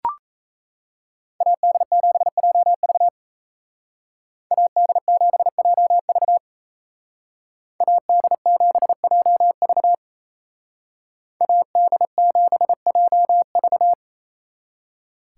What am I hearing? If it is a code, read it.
Morse code, heard as a keyed tone, about 36 wpm, 700 Hz. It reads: AD7JV AD7JV AD7JV AD7JV